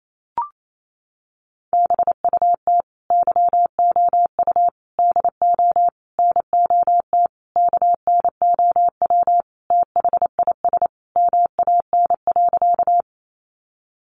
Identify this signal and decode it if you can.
Morse code, heard as a keyed tone, about 28 wpm, 700 Hz. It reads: BUT YOU DO NOT KNOW THIS MAN.
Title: Sentences from Top 600 Words 28wpm